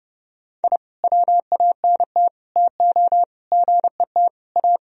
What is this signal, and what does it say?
Morse code, heard as a keyed tone, about 32 wpm, 700 Hz. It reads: I WANT TO GET A
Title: Sentences from Top 1000 Words - Review Part 1 - 30wpm